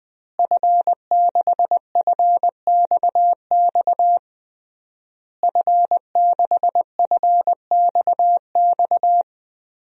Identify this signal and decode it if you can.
Morse code, heard as a keyed tone, about 20 wpm, 700 Hz. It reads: F6FXX F6FXX
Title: DXCC-100 Call Signs - 5 Characters - Repeated 3x - 20wpm